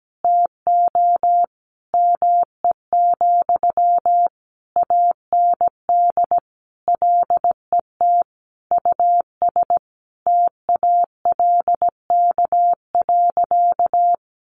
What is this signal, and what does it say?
Morse code, heard as a keyed tone, about 17 wpm, 700 Hz. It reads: TO ME, AND LET US TALK.
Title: Sentences from Top 1300 Words 17wpm